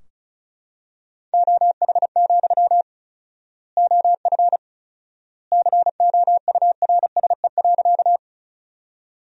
Morse code, heard as a keyed tone, about 34 wpm, 700 Hz. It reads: OH, OF COURSE.
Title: 4x Word-Spacing - Sentences from Top 500 Words - 35wpm